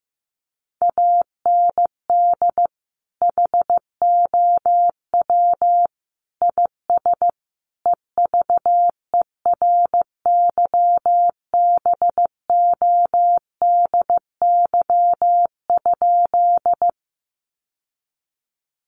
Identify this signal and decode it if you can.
Morse code, heard as a keyed tone, about 14 wpm, 700 Hz. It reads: AND HOW IS EVERYBODY?
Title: Sentences from Top 1400 Words 15wpm